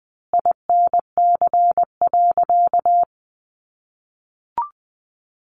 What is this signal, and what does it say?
Morse code, heard as a keyed tone, about 20 wpm, 700 Hz. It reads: INC.
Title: A Sound Of Thunder 20wpm